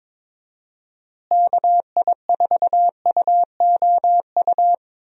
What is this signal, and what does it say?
Morse code, heard as a keyed tone, about 22 wpm, 700 Hz. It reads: KI4UOU
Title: USA Call Signs - 6 Characters - Repeated 3x - 22wpm